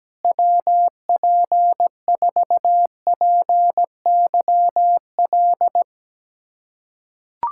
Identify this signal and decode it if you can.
Morse code, heard as a keyed tone, about 17 wpm, 700 Hz. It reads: WP4PYL
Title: USA Callsign DE Callsign - 6 Characters - 17wpm